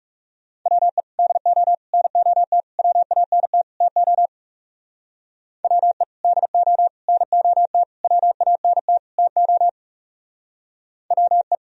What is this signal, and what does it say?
Morse code, heard as a keyed tone, about 36 wpm, 700 Hz. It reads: WE DO NOT WANT TO WE DO NOT WANT TO WE
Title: Speed-Racing - Sets of 5 Words 30wpm